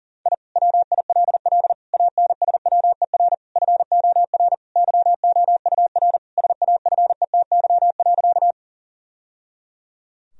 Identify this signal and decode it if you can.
Morse code, heard as a keyed tone, about 40 wpm, 700 Hz. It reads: I WILL ANSWER FOR YOUR SAFETY.
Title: Sentences from Top 1000 Words - Review Part 4 - 40wpm